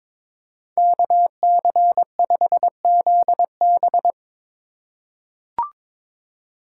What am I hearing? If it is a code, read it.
Morse code, heard as a keyed tone, about 22 wpm, 700 Hz. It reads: KC5ZB
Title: USA Call Signs - 5 Characters - 3x Character Spacing - 22wpm